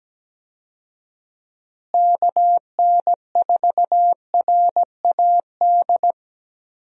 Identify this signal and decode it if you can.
Morse code, heard as a keyed tone, about 17 wpm, 700 Hz. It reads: KN4RAD
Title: USA Call Signs - 6 Characters - Repeated 2x - 17wpm